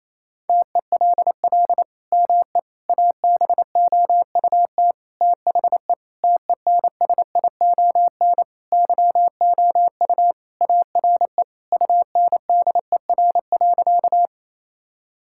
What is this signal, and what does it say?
Morse code, heard as a keyed tone, about 28 wpm, 700 Hz. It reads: TELL ME ABOUT THE TENSION YOU ARE UNDER.
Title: Sentences from Top 1600 Words 28wpm